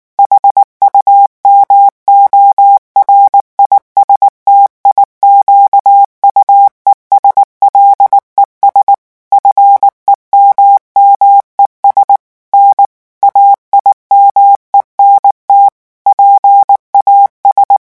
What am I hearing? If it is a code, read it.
Morse code, heard as a keyed tone, about 19 wpm, 800 Hz. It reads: HUMORISTIQUES LES FEMMES N AIMENT PAS
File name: morse mp3 / revision9 citations humoristiques vitesse 18 mots minute